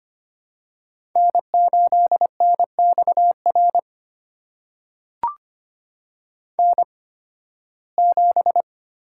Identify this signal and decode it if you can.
Morse code, heard as a keyed tone, about 25 wpm, spaced as for 16 wpm, 700 Hz. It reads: N8NXR N7
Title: USA Call Signs - 5 Characters - 8x Character Spacing - 25wpm